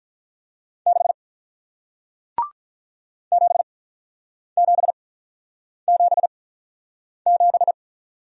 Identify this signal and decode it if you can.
Morse code, heard as a keyed tone, about 46 wpm, spaced as for 9 wpm, 700 Hz. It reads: 6 7777